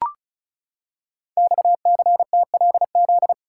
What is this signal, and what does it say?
Morse code, heard as a keyed tone, about 36 wpm, 700 Hz. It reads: XCTLZ
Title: Code Groups - 5 Letters - 35wpm